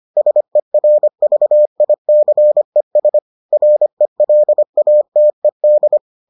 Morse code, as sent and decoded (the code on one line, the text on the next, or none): ... .-. ...- .. -.-. ... / .-. .-.. .- - . -..
SERVICES RELATED